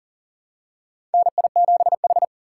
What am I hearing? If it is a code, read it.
Morse code, heard as a keyed tone, about 40 wpm, 700 Hz. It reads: NI7H
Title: USA Call Signs - 4 Characters - 6x Character Spacing - 40wpm